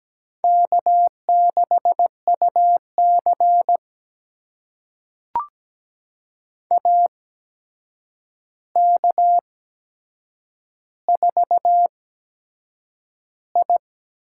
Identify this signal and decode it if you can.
Morse code, heard as a keyed tone, about 17 wpm, spaced as for 7 wpm, 700 Hz. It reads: K6UC AK4I
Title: USA Call Signs - 4 Characters - 8x Character Spacing - 17wpm